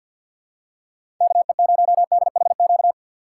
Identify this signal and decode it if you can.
Morse code, heard as a keyed tone, about 46 wpm, 700 Hz. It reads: KE0DHQ
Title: USA Call Signs - 6 Characters - 6x Character Spacing - 50wpm